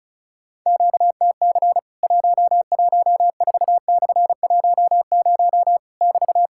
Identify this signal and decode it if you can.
Morse code, heard as a keyed tone, about 36 wpm, 700 Hz. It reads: QTC 114/10 <BT>
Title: Worked All Europe DX Contest - QTC Exchange - 35wpm